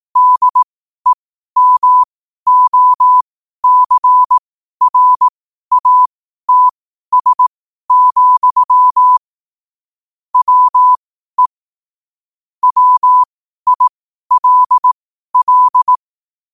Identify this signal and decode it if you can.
Morse code, heard as a keyed tone, about 18 wpm, spaced as for 12 wpm, 1000 Hz. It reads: DEMOCRATS, WE WILL
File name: qotdfiles / 12wpm